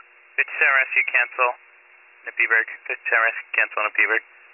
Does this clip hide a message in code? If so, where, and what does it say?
Yes: Emergency Alert System (SAME) tones from 0.6 to 4.3 s, heard as harsh buzzing bursts, with speech.